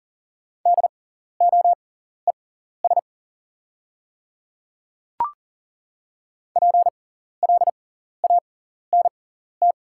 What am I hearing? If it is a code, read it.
Morse code, heard as a keyed tone, about 40 wpm, spaced as for 14 wpm, 690 Hz. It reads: DOES PLANT
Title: 6x Character Spacing - Top 300 Words - 40wpm